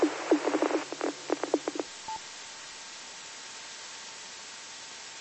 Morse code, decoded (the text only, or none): none